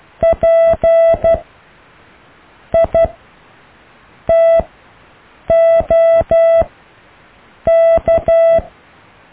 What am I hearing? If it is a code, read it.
Morse code, heard as a keyed tone, about 12 wpm, spaced as for 6 wpm, 660 Hz. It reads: P ITOK
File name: cw / EKTPV+BFHNW